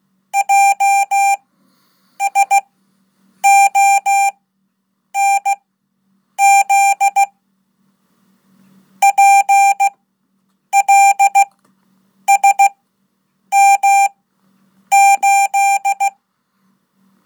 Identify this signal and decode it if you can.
Morse code, heard as a keyed tone, about 16 wpm, spaced as for 8 wpm, 790 Hz. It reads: JSONZ PLSM8
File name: Code / 5min-1